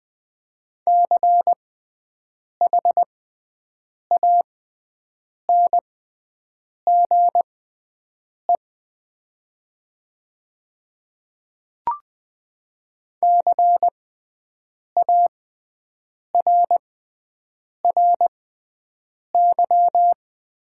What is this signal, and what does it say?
Morse code, heard as a keyed tone, about 20 wpm, spaced as for 7 wpm, 700 Hz. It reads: CHANGE CARRY
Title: 6x Character Spacing - Top 300 Words - 20wpm